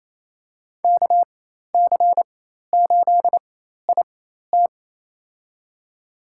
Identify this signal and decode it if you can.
Morse code, heard as a keyed tone, about 28 wpm, spaced as for 13 wpm, 700 Hz. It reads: KC8IT
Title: USA Call Signs - 4x Character Spacing - 28wpm